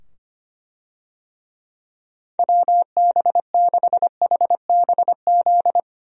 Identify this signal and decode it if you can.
Morse code, heard as a keyed tone, about 25 wpm, 700 Hz. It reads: WB6HBZ
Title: USA Call Signs - 6 Characters - 25wpm